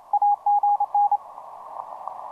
Morse code, heard as a keyed tone, about 29 wpm, 820 Hz, with radio noise. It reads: AGN
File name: 20m cw